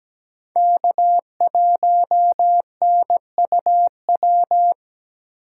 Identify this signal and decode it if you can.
Morse code, heard as a keyed tone, about 21 wpm, 700 Hz. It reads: K1NUW
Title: Speed Racing - USA 1x3 Call Signs 17wpm